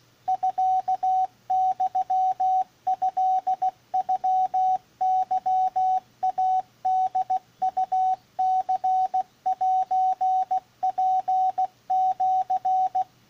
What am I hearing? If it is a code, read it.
Morse code (Japanese Wabun code), heard as a keyed tone, about 16 wpm, 750 Hz, with radio noise. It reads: ミユトノケイホウニセツシ